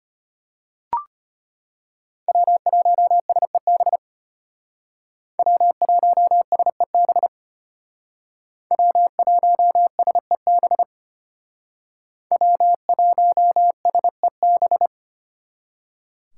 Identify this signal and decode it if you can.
Morse code, heard as a keyed tone, about 32 wpm, 700 Hz. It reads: W1SEB W1SEB W1SEB W1SEB